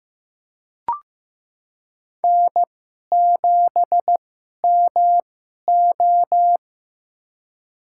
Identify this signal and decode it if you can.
Morse code, heard as a keyed tone, about 15 wpm, spaced as for 11 wpm, 700 Hz. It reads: N7MO
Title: USA Call Signs - 4 Characters - 2x Character Spacing - 15wpm